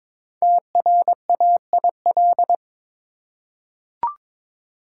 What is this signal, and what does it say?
Morse code, heard as a keyed tone, about 22 wpm, 700 Hz. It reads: TRAIL
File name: 5 Letter Words - First 3 Letters Given - 22wpm